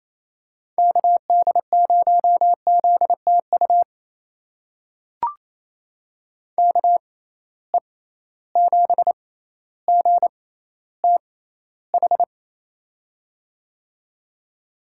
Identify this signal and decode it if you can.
Morse code, heard as a keyed tone, about 28 wpm, spaced as for 14 wpm, 700 Hz. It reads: KD0ZTU KE7GTH
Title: USA Call Signs - 6 Characters - 6x Character Spacing - 28wpm